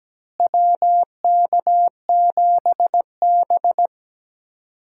Morse code, heard as a keyed tone, about 17 wpm, 700 Hz. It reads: WK7B
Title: USA Call Signs - 4 Characters - Repeated 3x - 17wpm